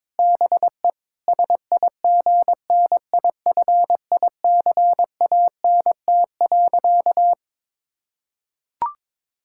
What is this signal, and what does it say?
Morse code, heard as a keyed tone, about 21 wpm, 700 Hz. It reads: BE SIGNIFICANT.
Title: Sentences from Top 1000 Words - Review Part 6 - 22wpm